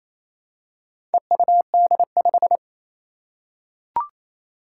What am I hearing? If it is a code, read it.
Morse code, heard as a keyed tone, about 27 wpm, 700 Hz. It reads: EUD5
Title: Four Challenging Letter-Number Combinations 28wpm